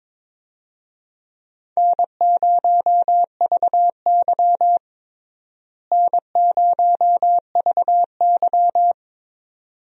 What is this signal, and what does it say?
Morse code, heard as a keyed tone, about 22 wpm, 700 Hz. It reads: N0VY N0VY
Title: USA Call Signs - 4 Characters - Repeated 2x - 22wpm